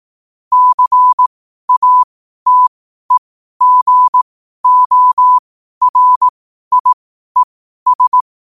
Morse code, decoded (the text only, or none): CATEGORIES